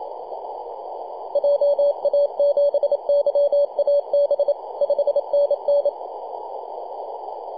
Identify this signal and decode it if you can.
Morse code, heard as a keyed tone, about 27 wpm, 570 Hz, with radio noise. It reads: JA7YAB 5NN
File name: 2229z 4W/JI1AVY 21.023 CW